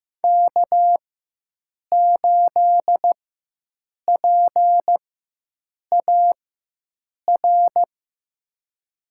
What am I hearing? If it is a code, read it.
Morse code, heard as a keyed tone, about 15 wpm, spaced as for 7 wpm, 700 Hz. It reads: K8PAR
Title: USA Call Signs - 4x Character Spacing - 15wpm